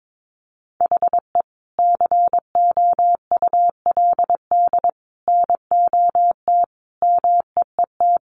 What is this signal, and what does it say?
Morse code, heard as a keyed tone, about 22 wpm, 700 Hz. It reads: HE COULD NOT MEET
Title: Sentences from Top 1100 Words 22wpm